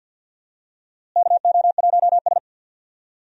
Morse code, heard as a keyed tone, about 52 wpm, 700 Hz. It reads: KO1S
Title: USA Call Signs - 4 Characters - Repeated 3x - 50wpm